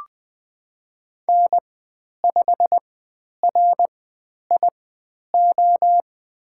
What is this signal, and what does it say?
Morse code, heard as a keyed tone, about 20 wpm, spaced as for 10 wpm, 700 Hz. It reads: N5RIO
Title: 109 Lesson 18g - Letter m - Call Signs 20wpm